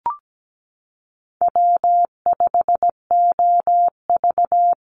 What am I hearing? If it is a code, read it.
Morse code, heard as a keyed tone, about 17 wpm, 700 Hz. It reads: W5OV
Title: Call Signs - 17wpm